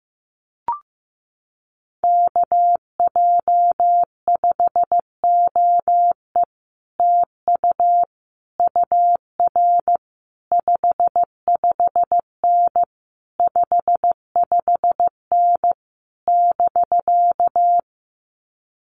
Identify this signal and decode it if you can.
Morse code, heard as a keyed tone, about 15 wpm, 700 Hz. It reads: KJ5OE TU UR 55N 55N <BK>
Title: Common QSO Phrases Encore 15wpm